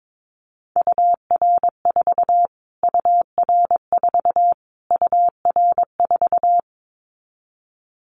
Morse code, heard as a keyed tone, about 22 wpm, 700 Hz. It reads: UR4 UR4 UR4
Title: Callsign Prefixes for Top 25 DXCC - Repeated 3x - 22wpm